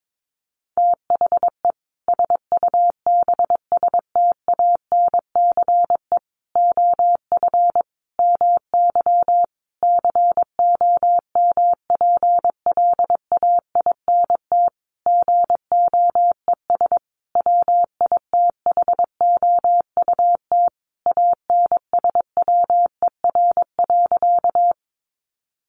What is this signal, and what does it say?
Morse code, heard as a keyed tone, about 22 wpm, 700 Hz. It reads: THE SUBSTANCE OF MY COMPLAINT GOES WITHOUT ANSWER.
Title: Sentences from Top 1500 Words 22wpm